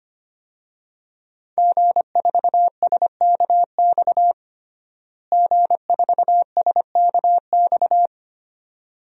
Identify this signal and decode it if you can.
Morse code, heard as a keyed tone, about 25 wpm, 700 Hz. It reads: G4SKX G4SKX
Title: DXCC-100 Call Signs - 5 Characters - Repeated 2x - 25wpm